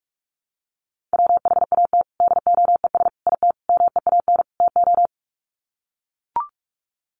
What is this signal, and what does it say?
Morse code, heard as a keyed tone, about 44 wpm, 700 Hz. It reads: WHAT DOES IT MEAN TO
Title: Sets of 5 Words Encore2 45wpm